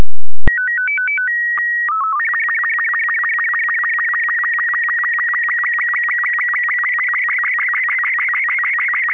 SSTV (slow-scan television), Robot 36.
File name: robot36 01